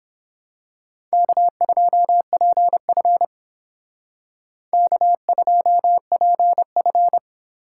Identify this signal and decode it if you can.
Morse code, heard as a keyed tone, about 26 wpm, 700 Hz. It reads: K2PF K2PF
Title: Speed Racing - USA Call Signs 22wpm